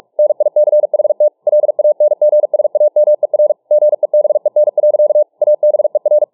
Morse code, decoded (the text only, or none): NICHT LANGSAMER GEBEN. ABER